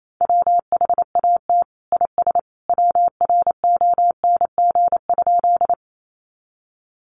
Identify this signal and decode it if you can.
Morse code, heard as a keyed tone, about 28 wpm, 690 Hz. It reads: WHAT IS WRONG?
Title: Sentences from Top 900 Words 28wpm